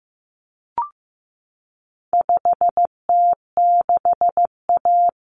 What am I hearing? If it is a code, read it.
Morse code, heard as a keyed tone, about 15 wpm, 700 Hz. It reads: 5T6A
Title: Common Cut Numbers - 4-Digit 15wpm